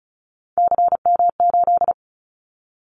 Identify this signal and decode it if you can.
Morse code, heard as a keyed tone, about 36 wpm, 700 Hz. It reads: CM8
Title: Callsign Prefixes for Top 100 DXCC - 35wpm